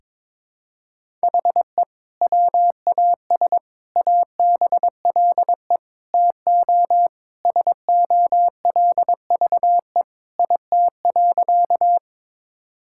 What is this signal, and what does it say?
Morse code, heard as a keyed tone, about 22 wpm, 700 Hz. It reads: HE WAS ABLE TO SOLVE IT.
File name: practice / 22wpm / Sentences with 3-Word Phrases - 22wpm